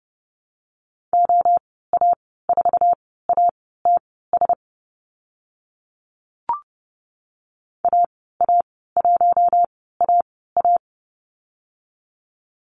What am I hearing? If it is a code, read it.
Morse code, heard as a keyed tone, about 30 wpm, spaced as for 17 wpm, 700 Hz. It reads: OA4ATS AA1AA
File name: Lesson 15g - Number 5 - Call Signs 30wpm